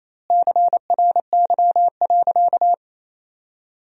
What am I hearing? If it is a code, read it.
Morse code, heard as a keyed tone, about 28 wpm, 700 Hz. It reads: CRY.
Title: 2x Word-Spacing - Sentences from Top 500 Words - 28wpm